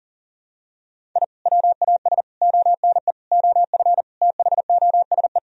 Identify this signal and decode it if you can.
Morse code, heard as a keyed tone, about 40 wpm, 700 Hz. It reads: I WAS ONE OF THOSE
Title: Sentences from Top 1000 Words - Review Part 1 - 40wpm